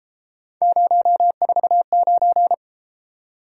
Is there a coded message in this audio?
Yes: Morse code from 0.6 to 2.5 s, heard as a keyed tone, about 30 wpm, 700 Hz. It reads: 049